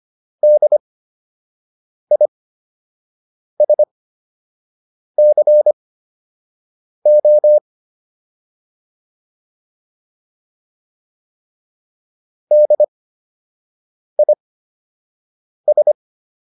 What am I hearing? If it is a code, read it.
Morse code, heard as a keyed tone, about 25 wpm, spaced as for 6 wpm, 600 Hz. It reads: DISCO DIS